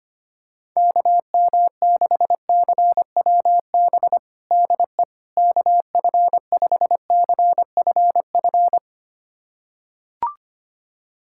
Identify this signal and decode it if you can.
Morse code, heard as a keyed tone, about 25 wpm, 700 Hz. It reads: KM6CWB DE KF5CFF
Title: USA Callsign DE Callsign 25wpm